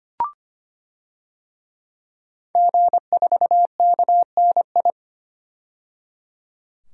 Morse code, heard as a keyed tone, about 25 wpm, 700 Hz. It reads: G4KNI
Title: DXCC-100 Call Signs - 5 Characters - 25wpm